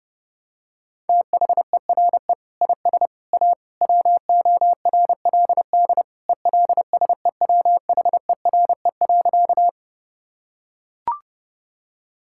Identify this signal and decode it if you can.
Morse code, heard as a keyed tone, about 32 wpm, 700 Hz. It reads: THERE IS A WORLD ELSEWHERE.
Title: Sentences from Top 1600 Words 30wpm